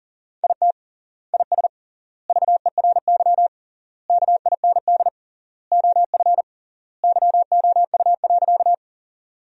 Morse code, heard as a keyed tone, about 40 wpm, 700 Hz. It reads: IT IS VERY KIND OF YOU.
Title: 3x Word-Spacing - Sentences from Top 200 Words - 40wpm